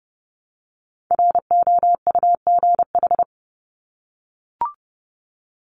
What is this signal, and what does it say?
Morse code, heard as a keyed tone, about 30 wpm, 700 Hz. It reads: ROUGH
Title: Top 1501-1750 Words 30wpm